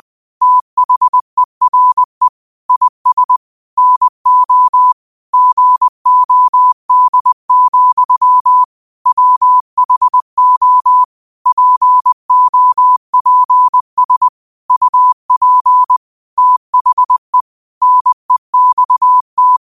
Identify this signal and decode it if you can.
Morse code, heard as a keyed tone, about 20 wpm, 1000 Hz. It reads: THERE IS NO GOD, WHO POPS UP THE NEXT
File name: qotdfiles / 20wpm